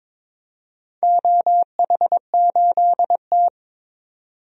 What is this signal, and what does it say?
Morse code, heard as a keyed tone, about 23 wpm, 700 Hz. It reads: OH8T
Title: DXCC-100 Call Signs - 4 Characters - 22wpm